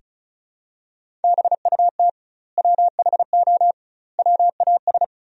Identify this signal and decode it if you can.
Morse code, heard as a keyed tone, about 36 wpm, 700 Hz. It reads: BUT WHO WAS
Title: 2x Word-Spacing - Sentences from Top 100 Words - 35wpm